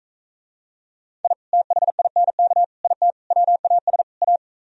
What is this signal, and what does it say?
Morse code, heard as a keyed tone, about 44 wpm, 700 Hz. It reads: I THINK IT WAS A